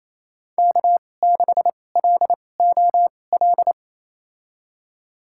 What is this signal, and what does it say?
Morse code, heard as a keyed tone, about 28 wpm, spaced as for 20 wpm, 700 Hz. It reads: K6LOL
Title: USA Call Signs - 5 Characters - 2x Character Spacing - 28wpm